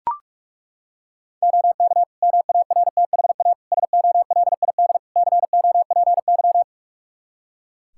Morse code, heard as a keyed tone, about 41 wpm, 700 Hz. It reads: OK MARTHA SOLID COPY
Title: Practice QSO - Exchange 3 - More Info - 45wpm